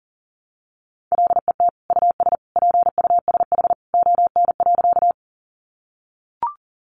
Morse code, heard as a keyed tone, about 40 wpm, 700 Hz. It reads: LET US PUSH ON.
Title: Sentences from Top 1300 Words 40wpm